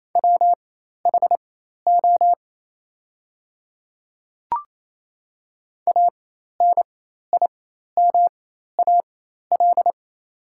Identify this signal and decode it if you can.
Morse code, heard as a keyed tone, about 28 wpm, spaced as for 13 wpm, 700 Hz. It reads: WHO ANIMAL
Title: 4x Character Spacing - Top 200 Words - 28wpm